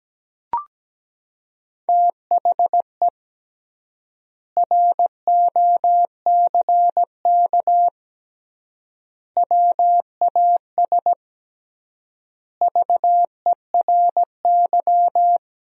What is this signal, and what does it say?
Morse code, heard as a keyed tone, about 17 wpm, 700 Hz. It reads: THE ROCK WAS VERY
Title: 3x Word-Spacing - Sentences from Top 400 Words - 17wpm